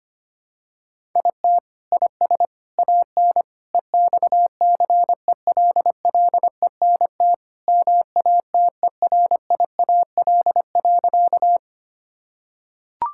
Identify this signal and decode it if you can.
Morse code, heard as a keyed tone, about 25 wpm, 700 Hz. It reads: IT IS AN EXCELLENT MATERIAL.
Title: Sentences from Top 1000 Words - Review Part 5 - 25wpm